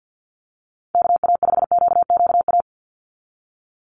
Morse code, heard as a keyed tone, about 48 wpm, 700 Hz. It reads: KA5QQA
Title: NA SOTA Chaser Callsigns 50wpm